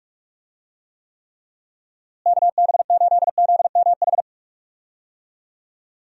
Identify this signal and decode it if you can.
Morse code, heard as a keyed tone, about 46 wpm, 700 Hz. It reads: KB8ZMH
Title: USA Call Signs - 6 Characters - Repeated 3x - 45wpm